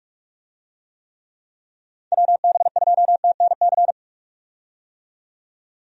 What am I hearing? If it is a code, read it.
Morse code, heard as a keyed tone, about 46 wpm, 700 Hz. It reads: WB2TNC
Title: USA Call Signs - 6 Characters - 45wpm